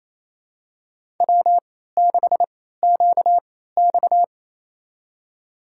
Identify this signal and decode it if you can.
Morse code, heard as a keyed tone, about 28 wpm, spaced as for 16 wpm, 700 Hz. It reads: W6QX
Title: USA Call Signs - 4 Characters - 3x Character Spacing - 28wpm